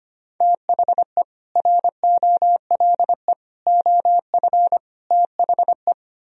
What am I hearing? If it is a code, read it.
Morse code, heard as a keyed tone, about 33 wpm, 700 Hz. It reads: THE ROLE OF THE